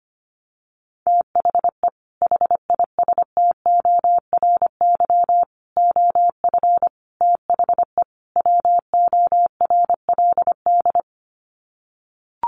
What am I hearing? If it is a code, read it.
Morse code, heard as a keyed tone, about 25 wpm, 700 Hz. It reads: THE HISTORY OF THE WORLD
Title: Sets of 5 Words Encore2 25wpm